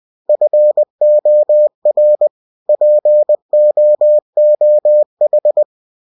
Morse code, heard as a keyed tone, about 20 wpm, 600 Hz. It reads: FOR POOH
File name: cw-books / winnie-the-pooh / winnie-20wpm-page-0033